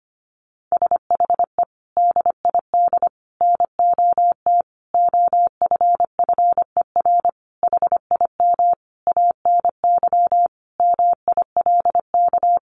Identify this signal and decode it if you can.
Morse code, heard as a keyed tone, about 25 wpm, 700 Hz. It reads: SHE DID NOT OFFER HIM ANY MILK
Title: Sentences from Top 1500 Words 25wpm